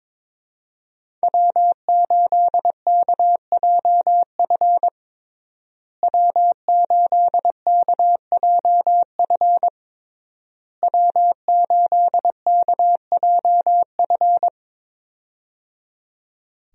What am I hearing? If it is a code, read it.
Morse code, heard as a keyed tone, about 22 wpm, 700 Hz. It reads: W8KJF W8KJF W8KJF
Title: USA Call Signs - 5 Characters - Repeated 3x - 22wpm